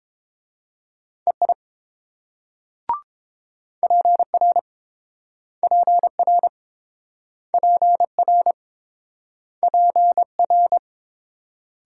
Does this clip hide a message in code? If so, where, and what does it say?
Yes: Morse code from 1.3 to 10.8 s, heard as a keyed tone, about 28 wpm, 700 Hz. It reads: EI PR PR PR PR